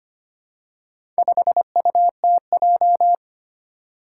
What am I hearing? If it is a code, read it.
Morse code, heard as a keyed tone, about 25 wpm, 700 Hz. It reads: 5UTJ
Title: Four Challenging Letter-Number Combinations 25wpm